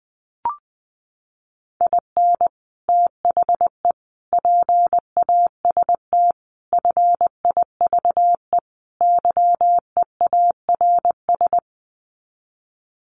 Morse code, heard as a keyed tone, about 20 wpm, 700 Hz. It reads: IN THE PAST FIVE YEARS
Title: Sets of 5 Words Encore 20wpm